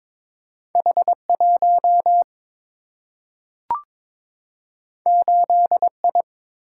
Morse code, heard as a keyed tone, about 22 wpm, 700 Hz. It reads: H 1 8 I
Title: Single-Letter Single-Number 22wpm